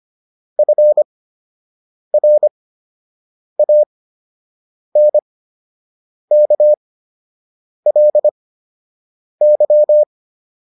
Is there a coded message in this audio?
Yes: Morse code from 0.6 to 10.0 s, heard as a keyed tone, about 25 wpm, spaced as for 7 wpm, 600 Hz. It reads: FRANKLY